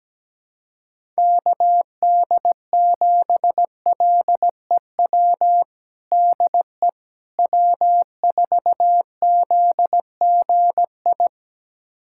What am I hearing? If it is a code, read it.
Morse code, heard as a keyed tone, about 17 wpm, 700 Hz. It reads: KD7LEW DE W4ZGI
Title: USA Callsign DE Callsign 17wpm